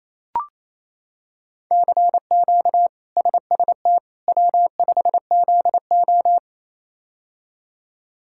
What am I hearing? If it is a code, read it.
Morse code, heard as a keyed tone, about 28 wpm, 700 Hz. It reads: CQ SST W5ZO
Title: K1USN SST QSO Practice - 28wpm